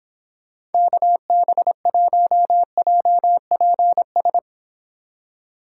Morse code, heard as a keyed tone, about 31 wpm, 700 Hz. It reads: KB1JPS